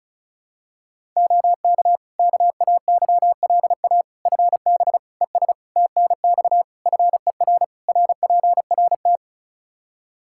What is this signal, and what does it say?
Morse code, heard as a keyed tone, about 34 wpm, 700 Hz. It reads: OK KAYLA FB ES TNX FER RPRT
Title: K1USN SST QSO Practice 35wpm